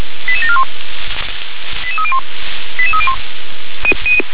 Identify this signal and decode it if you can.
Five-tone selective call (ZVEI), heard as quick runs of five tones, tones of about 70 ms. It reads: 88741 88301 88301